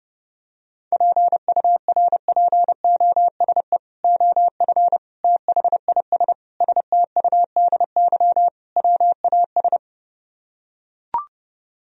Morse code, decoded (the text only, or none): PURPOSE OF THIS STUDY WAS